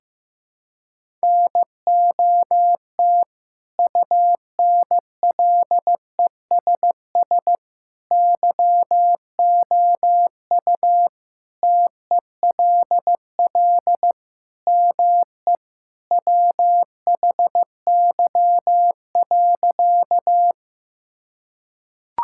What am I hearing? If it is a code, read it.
Morse code, heard as a keyed tone, about 16 wpm, 700 Hz. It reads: NOT UNLESS YOU TELL ME WHY.
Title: Sentences from Top 1000 Words - Review Part 6 - 15wpm